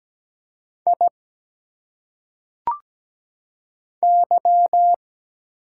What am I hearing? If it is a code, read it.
Morse code, heard as a keyed tone, about 17 wpm, 700 Hz. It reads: I Y
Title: Single Letters - Phonetic Pronunciation 17wpm